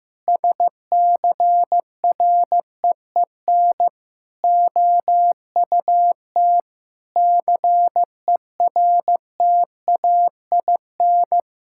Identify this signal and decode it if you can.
Morse code, heard as a keyed tone, about 15 wpm, 700 Hz. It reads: SCREEN OUT CERTAIN
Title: Sentences from Top 1000 Words - Review Part 6 - 15wpm